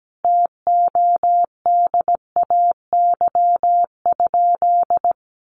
Morse code, heard as a keyed tone, about 17 wpm, 700 Hz. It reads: TODAY?
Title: Sentences from Top 1500 Words 17wpm